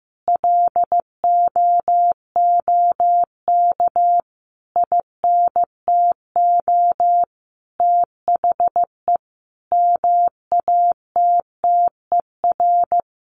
Morse code, heard as a keyed tone, about 15 wpm, 700 Hz. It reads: LOOK INTO THE MATTER
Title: Sentences from Top 1400 Words 15wpm